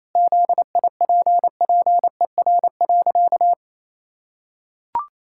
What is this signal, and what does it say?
Morse code, heard as a keyed tone, about 27 wpm, 700 Hz. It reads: ZIPPER.
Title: Pangram Challenge v2 - 28wpm